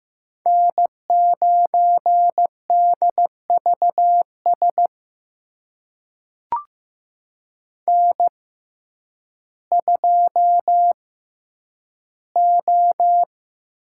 Morse code, heard as a keyed tone, about 15 wpm, spaced as for 9 wpm, 700 Hz. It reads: N9DVS N2O
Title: USA Call Signs - 5 Characters - 6x Character Spacing - 15wpm